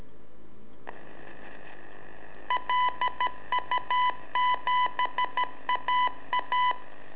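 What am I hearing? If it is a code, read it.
Morse code, heard as a keyed tone, about 13 wpm, 1000 Hz. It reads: LU7AA
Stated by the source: Telemetry in Morse from LU7AA and LU4EZ, AMSAT and Turma Mar del Plata, sent from a balloon.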